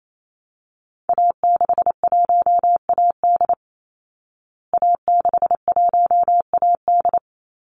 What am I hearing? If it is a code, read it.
Morse code, heard as a keyed tone, about 28 wpm, 700 Hz. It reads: A61AD A61AD
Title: Call Signs Repeated 2x - 28wpm